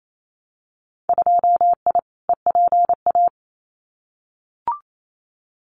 Morse code, heard as a keyed tone, about 28 wpm, 700 Hz. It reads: 2I EPA
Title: Winter Field Day Exchanges 28wpm